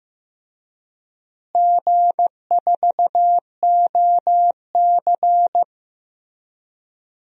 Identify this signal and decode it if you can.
Morse code, heard as a keyed tone, about 15 wpm, 700 Hz. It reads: G4OC
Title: DXCC-100 Call Signs - 4 Characters - Repeated 3x - 15wpm